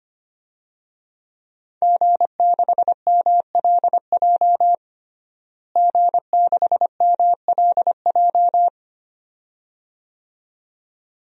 Morse code, heard as a keyed tone, about 25 wpm, 700 Hz. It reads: G6MLJ G6MLJ
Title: DXCC-100 Call Signs - 5 Characters - Repeated 2x - 25wpm